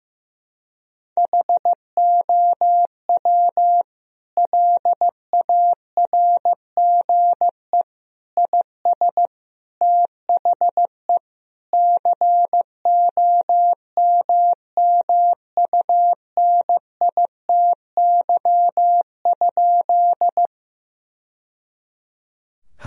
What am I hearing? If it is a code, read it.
Morse code, heard as a keyed tone, about 15 wpm, 700 Hz. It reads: HOW LARGE IS THE COMMUNITY?
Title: Sentences from Top 1000 Words - Review Part 4 - 15wpm